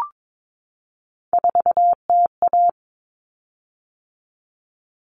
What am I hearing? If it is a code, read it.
Morse code, heard as a keyed tone, about 22 wpm, 700 Hz. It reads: 4TA
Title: Common Cut Numbers - 3-Digit 22wpm